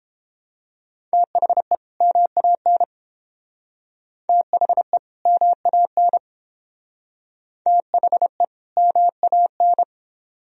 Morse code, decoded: THE MAN THE MAN THE MAN

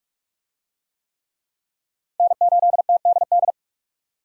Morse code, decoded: N8TDD

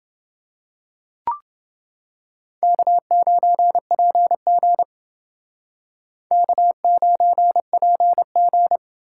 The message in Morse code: K9PG K9PG

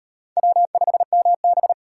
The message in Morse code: W5MB

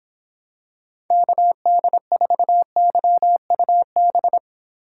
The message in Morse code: KD4YUB

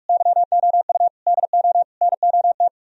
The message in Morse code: YOU DO NOT

